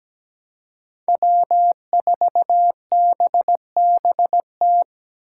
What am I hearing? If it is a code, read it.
Morse code, heard as a keyed tone, about 17 wpm, 700 Hz. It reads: W4BBT
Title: K1USN SST QSO Practice - 17wpm